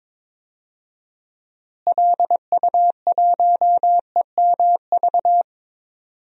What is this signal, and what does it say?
Morse code, heard as a keyed tone, about 21 wpm, 700 Hz. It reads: LU1EMV
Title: DXCC-100 Call Signs - 6 Characters - Repeated 2x - 22wpm